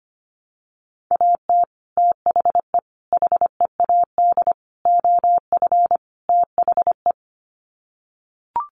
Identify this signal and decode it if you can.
Morse code, heard as a keyed tone, about 25 wpm, 700 Hz. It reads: AT THE HEAD OF THE
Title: Sets of 5 Words 25wpm